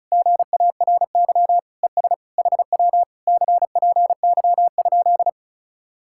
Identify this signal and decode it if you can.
Morse code, heard as a keyed tone, about 40 wpm, 700 Hz. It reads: GARY ES HW CPY?